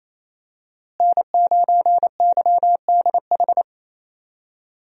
Morse code, heard as a keyed tone, about 27 wpm, 700 Hz. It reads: N9YDH
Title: USA Call Signs - 5 Characters - Repeated 3x - 28wpm